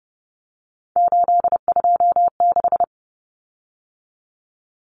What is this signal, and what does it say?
Morse code, heard as a keyed tone, about 30 wpm, 700 Hz. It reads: 826
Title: Common Cut Numbers - 3-Digit 30wpm